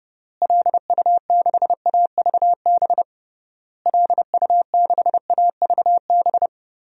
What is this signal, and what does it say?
Morse code, heard as a keyed tone, about 32 wpm, 700 Hz. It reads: LU6AVB LU6AVB
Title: DXCC-100 Call Signs - 6 Characters - Repeated 3x - 30wpm